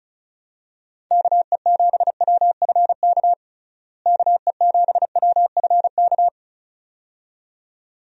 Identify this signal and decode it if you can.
Morse code, heard as a keyed tone, about 36 wpm, 700 Hz. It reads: KE7WFK KE7WFK
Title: USA Call Signs - 6 Characters - Repeated 2x - 35wpm